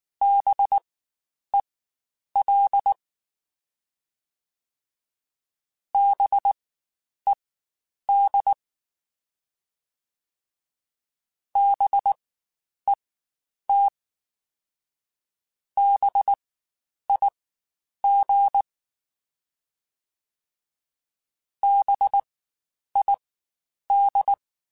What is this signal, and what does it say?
Morse code, heard as a keyed tone, about 19 wpm, spaced as for 9 wpm, 800 Hz. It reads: BEL BED BET BIG BID
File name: Spaced Alphas / Spaced Alphas mp3s / Bs sp @18